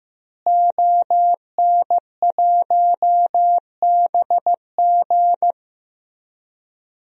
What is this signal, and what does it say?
Morse code, heard as a keyed tone, about 16 wpm, 700 Hz. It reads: ON1BG
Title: DXCC-100 Call Signs - 5 Characters - Repeated 3x - 15wpm